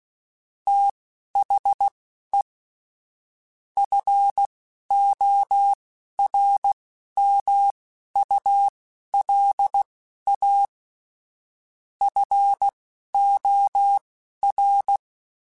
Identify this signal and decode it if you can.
Morse code, heard as a keyed tone, about 16 wpm, spaced as for 11 wpm, 800 Hz. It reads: THE FORMULA FOR